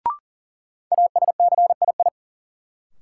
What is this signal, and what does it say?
Morse code, heard as a keyed tone, about 40 wpm, 700 Hz. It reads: ASCII
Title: Sets of 5 Words - New Word Drill - 40wpm